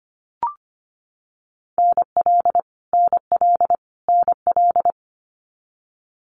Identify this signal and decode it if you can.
Morse code, heard as a keyed tone, about 25 wpm, 700 Hz. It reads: NL NL NL
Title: Canadian Provinces and Territories - Repeated 3x - 25wpm